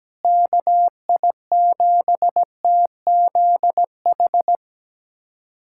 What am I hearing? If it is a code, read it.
Morse code, heard as a keyed tone, about 17 wpm, 700 Hz. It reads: KI7TZH
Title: USA Callsign DE Callsign - 6 Characters - 17wpm